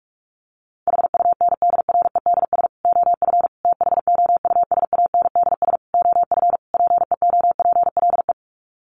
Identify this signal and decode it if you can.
Morse code, heard as a keyed tone, about 44 wpm, 700 Hz. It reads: HUNDREDS OF THOUSANDS OF PEOPLE
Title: Sets of 5 Words Encore2 45wpm